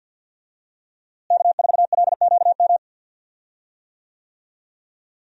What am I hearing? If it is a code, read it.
Morse code, heard as a keyed tone, about 50 wpm, 700 Hz. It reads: K4LQM
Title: USA Call Signs - 5 Characters - Repeated 3x - 50wpm